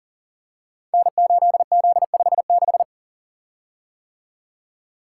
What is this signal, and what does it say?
Morse code, heard as a keyed tone, about 38 wpm, 700 Hz. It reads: N8Z56
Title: Code Groups - 5 Letter:Numbers - 40wpm